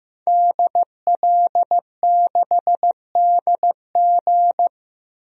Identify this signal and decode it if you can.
Morse code, heard as a keyed tone, about 15 wpm, 700 Hz. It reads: DL6DG
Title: DXCC-100 Call Signs - 5 Characters - Repeated 3x - 15wpm